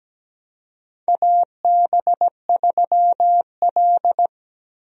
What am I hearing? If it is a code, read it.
Morse code, heard as a keyed tone, about 17 wpm, 700 Hz. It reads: AB3L
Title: USA Call Signs - 4 Characters - Repeated 2x - 17wpm